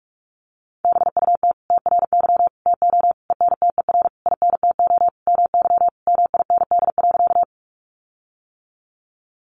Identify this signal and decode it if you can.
Morse code, heard as a keyed tone, about 44 wpm, 700 Hz. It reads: BUT TRY TO ENTER INTO MY MIND.
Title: Sentences from Top 1300 Words 45wpm